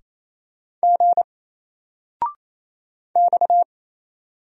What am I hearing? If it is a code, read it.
Morse code, heard as a keyed tone, about 27 wpm, 700 Hz. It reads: G X
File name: Single-Letter-Number-28wpm